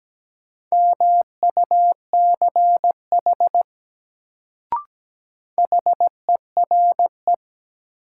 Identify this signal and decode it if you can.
Morse code, heard as a keyed tone, about 17 wpm, 700 Hz. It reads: MUCH HERE
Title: Sentences from Top 200 Words - New Word Drill - 17wpm